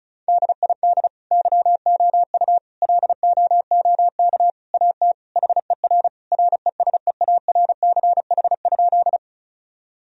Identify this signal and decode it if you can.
Morse code, heard as a keyed tone, about 35 wpm, 700 Hz. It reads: DID YOU LOOK AT HER RESEARCH?
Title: Sentences from Top 1000 Words - Review Part 4 - 35wpm